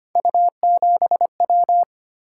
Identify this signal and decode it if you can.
Morse code, heard as a keyed tone, about 25 wpm, 700 Hz. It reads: U7W
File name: Three Letter-Number Combinations - 25wpm